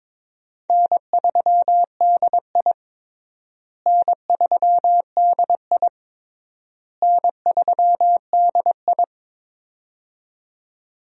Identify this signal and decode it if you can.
Morse code, heard as a keyed tone, about 22 wpm, 700 Hz. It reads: N3DI N3DI N3DI
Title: USA Call Signs - 4 Characters - Repeated 3x - 22wpm